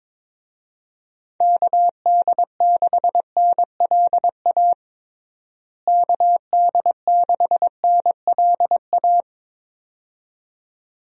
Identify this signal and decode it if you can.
Morse code, heard as a keyed tone, about 22 wpm, 700 Hz. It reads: KD6NLA KD6NLA
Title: USA Call Signs - 6 Characters - Repeated 2x - 22wpm